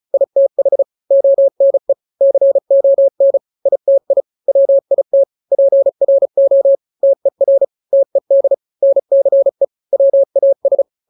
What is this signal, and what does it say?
Morse code, heard as a keyed tone, about 34 wpm, 550 Hz. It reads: ITH ONE CON ITI WIT PRO TER TED NCE WAS
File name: 3-grams 35 30 WPM